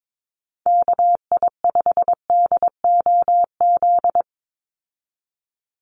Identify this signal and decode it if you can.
Morse code, heard as a keyed tone, about 22 wpm, 700 Hz. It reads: KI5DOZ
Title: USA Call Signs 22wpm